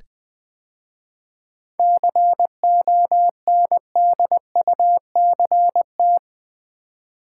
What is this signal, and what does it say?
Morse code, heard as a keyed tone, about 20 wpm, 700 Hz. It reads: CONDUCT
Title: 7 Letter Words - First 3 Letters Given - 20wpm